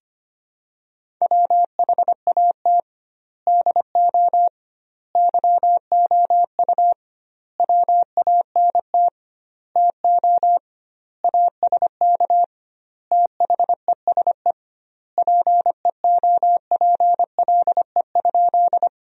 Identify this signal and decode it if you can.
Morse code, heard as a keyed tone, about 25 wpm, 700 Hz. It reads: WHAT DO YOU WANT TO ASK THESE PEOPLE?
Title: 2x Word-Spacing - Sentences from Top 200 Words - 25wpm